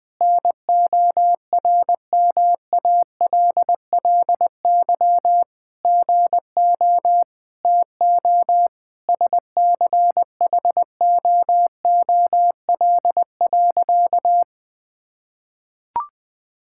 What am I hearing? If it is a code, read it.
Morse code, heard as a keyed tone, about 21 wpm, 700 Hz. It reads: NORMALLY GO TO SCHOOL.
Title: Sentences from Top 1000 Words - Review Part 6 - 20wpm